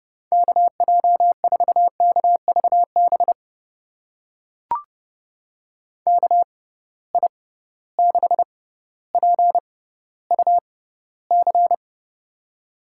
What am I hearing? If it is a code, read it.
Morse code, heard as a keyed tone, about 30 wpm, spaced as for 15 wpm, 700 Hz. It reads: KJ4KVB KI6PUC
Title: USA Call Signs - 6 Characters - 6x Character Spacing - 30wpm